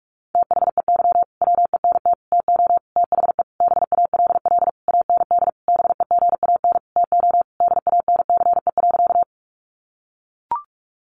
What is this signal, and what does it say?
Morse code, heard as a keyed tone, about 44 wpm, 700 Hz. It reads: THEY WENT TO THE BALL AND BEGAN TO DANCE.
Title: Sentences from Top 900 Words 45wpm